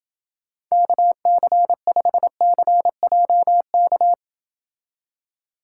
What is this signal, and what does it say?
Morse code, heard as a keyed tone, about 29 wpm, 700 Hz. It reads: KC5CJK